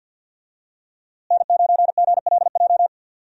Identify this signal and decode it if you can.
Morse code, heard as a keyed tone, about 50 wpm, 700 Hz. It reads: N9GLJ
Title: USA Call Signs - 5 Characters - Repeated 3x - 50wpm